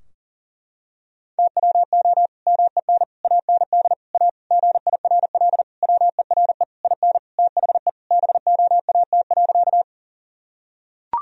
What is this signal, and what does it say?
Morse code, heard as a keyed tone, about 37 wpm, 700 Hz. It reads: TWO MEN AND A GIRL WERE IN THE BOAT.
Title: Sentences from Top 1000 Words - Review Part 3 - 40wpm